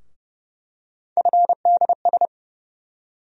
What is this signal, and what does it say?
Morse code, heard as a keyed tone, about 30 wpm, 700 Hz. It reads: FDS
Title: USA Call Sign Postfixes - 3 Letters 30wpm